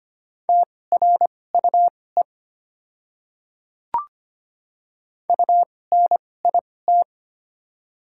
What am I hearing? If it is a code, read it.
Morse code, heard as a keyed tone, about 25 wpm, spaced as for 18 wpm, 700 Hz. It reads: TRUE UNIT
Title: 2x Character Spacing - Top 500 Words - 25wpm